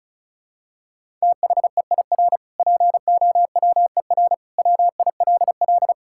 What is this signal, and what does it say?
Morse code, heard as a keyed tone, about 34 wpm, 700 Hz. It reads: THEIR POWER WILL
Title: Sentences from Top 1000 Words - Review Part 6 - 35wpm